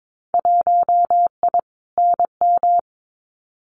Morse code, heard as a keyed tone, about 22 wpm, 700 Hz. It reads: 1I NM
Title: Winter Field Day Exchanges Encore 22wpm